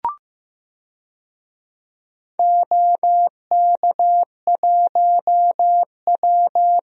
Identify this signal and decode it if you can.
Morse code, heard as a keyed tone, about 15 wpm, 700 Hz. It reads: OK1W
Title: DXCC-100 Call Signs - 4 Characters - 15wpm